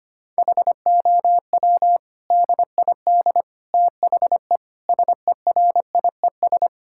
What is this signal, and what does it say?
Morse code, heard as a keyed tone, about 25 wpm, 700 Hz. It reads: HOW DID THE SERIES
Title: Sentences from Top 1000 Words - Review Part 5 - 25wpm